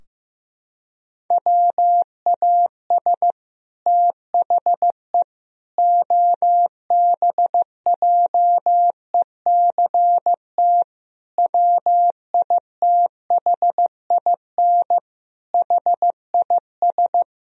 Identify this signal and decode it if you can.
Morse code, heard as a keyed tone, about 16 wpm, 700 Hz. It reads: WAS THE OBJECT WITHIN HIS
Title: Sentences from Top 1000 Words - Review Part 4 - 15wpm